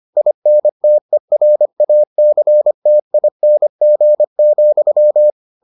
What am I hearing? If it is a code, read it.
Morse code, heard as a keyed tone, about 25 wpm, 600 Hz. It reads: INTERACTING,